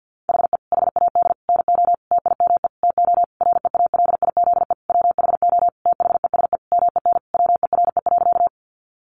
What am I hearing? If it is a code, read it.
Morse code, heard as a keyed tone, about 48 wpm, 700 Hz. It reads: HE HAD NO TIME TO REALIZE WHO THESE MEN WERE.
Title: Sentences from Top 1000 Words 50wpm